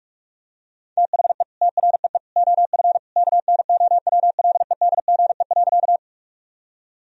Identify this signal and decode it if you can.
Morse code, heard as a keyed tone, about 46 wpm, 700 Hz. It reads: THE TREE OF KNOWLEDGE.
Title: Sentences from Top 1000 Words - Review Part 3 - 45wpm